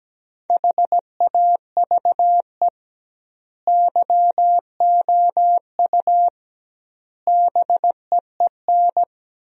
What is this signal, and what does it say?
Morse code, heard as a keyed tone, about 17 wpm, 700 Hz. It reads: HAVE YOU BEEN
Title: 2x Word-Spacing - Sentences from Top 200 Words - 17wpm